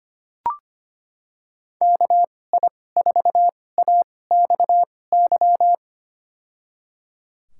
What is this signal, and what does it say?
Morse code, heard as a keyed tone, about 25 wpm, spaced as for 18 wpm, 700 Hz. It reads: KI4AXY
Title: USA Call Signs - 2x Character Spacing - 25wpm